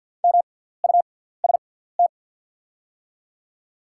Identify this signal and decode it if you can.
Morse code, heard as a keyed tone, about 48 wpm, spaced as for 17 wpm, 700 Hz. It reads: MUST
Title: 6x Character Spacing - Top 200 Words - 50wpm